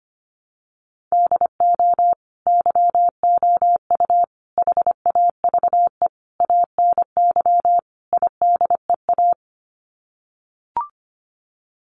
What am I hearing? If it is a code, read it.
Morse code, heard as a keyed tone, about 25 wpm, 700 Hz. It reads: DO YOU HAVE ANY IDEA
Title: Sets of 5 Words Encore 25wpm